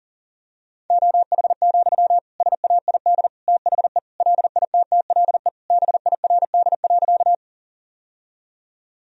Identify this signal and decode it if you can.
Morse code, heard as a keyed tone, about 37 wpm, 700 Hz. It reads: OH, SAID THE LITTLE BIRD.
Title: Sentences from Top 1000 Words - Review Part 2 - 40wpm